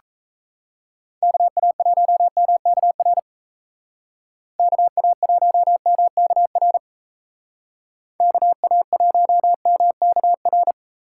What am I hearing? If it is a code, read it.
Morse code, heard as a keyed tone, about 33 wpm, 700 Hz. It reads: KA1MKR KA1MKR KA1MKR
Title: Speed Racing - USA 2x3 Call Signs 28wpm